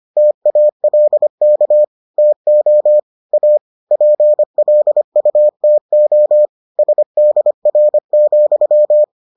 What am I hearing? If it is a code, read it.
Morse code, heard as a keyed tone, about 25 wpm, 600 Hz. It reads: TALK TO A PLUTO SDR,